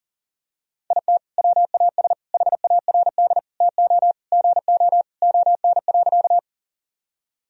Morse code, heard as a keyed tone, about 40 wpm, 700 Hz. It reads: IT WAS HARD TO GO ON.
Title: Sentences from Top 1000 Words - Review Part 1 - 40wpm